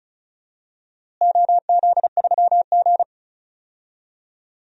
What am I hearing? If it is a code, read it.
Morse code, heard as a keyed tone, about 34 wpm, 700 Hz. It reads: OZ3G
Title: DXCC-100 Call Signs - 4 Characters - Repeated 2x - 35wpm